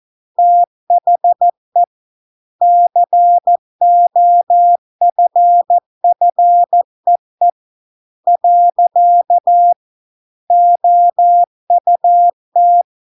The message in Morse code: THE COFFEE . OUT

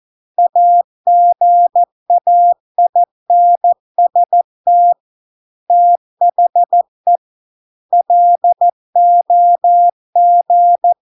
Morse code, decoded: AGAINST THE LOG

